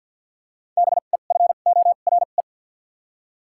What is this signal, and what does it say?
Morse code, heard as a keyed tone, about 53 wpm, spaced as for 37 wpm, 700 Hz. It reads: BEFORE
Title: 2x Character Spacing - Top 500 Words - 50wpm